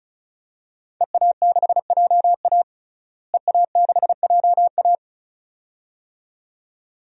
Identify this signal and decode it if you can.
Morse code, heard as a keyed tone, about 36 wpm, 700 Hz. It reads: EA6JA EA6JA
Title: DXCC-100 Call Signs - 5 Characters - Repeated 2x - 35wpm